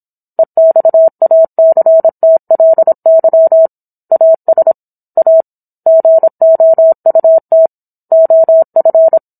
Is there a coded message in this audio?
Yes: Morse code from 0.4 to 9.2 s, heard as a keyed tone, about 27 wpm, 650 Hz. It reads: EXACTLY AS A GOUT OF